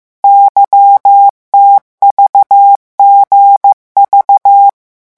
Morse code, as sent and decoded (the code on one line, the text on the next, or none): -.-- - ...- --. ...-
YTVGV